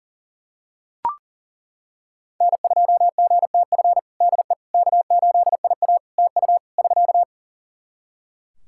Morse code, heard as a keyed tone, about 40 wpm, 700 Hz. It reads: N2GTF DE K8IA TU <SK>
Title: Practice QSO - Exchange 4 - Ending - 40wpm